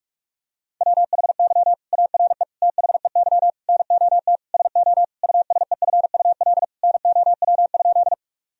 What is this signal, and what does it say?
Morse code, heard as a keyed tone, about 46 wpm, 700 Hz. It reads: WHY ARE THEY NOT SO USEFUL NOW?
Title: Sentences from Top 1000 Words - Review Part 4 - 45wpm